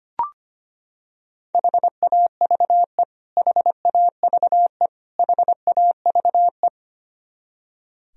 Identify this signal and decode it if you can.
Morse code, heard as a keyed tone, about 25 wpm, 700 Hz. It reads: HAVE HAVE HAVE
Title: Top 100 Words Repeated 25wpm